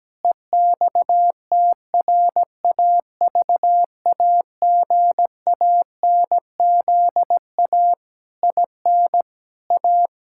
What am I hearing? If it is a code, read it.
Morse code, heard as a keyed tone, about 17 wpm, 700 Hz. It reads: EXTRAVAGANZA IN A
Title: Pangram Challenge v2 - 17wpm